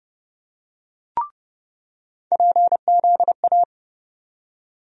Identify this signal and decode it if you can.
Morse code, heard as a keyed tone, about 32 wpm, 700 Hz. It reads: PZA
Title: USA Call Sign Postfixes - 3 Letters 30wpm